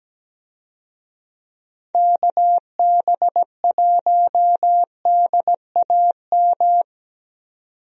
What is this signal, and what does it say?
Morse code, heard as a keyed tone, about 17 wpm, 700 Hz. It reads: KB1DAM
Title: USA Call Signs - 6 Characters - 17wpm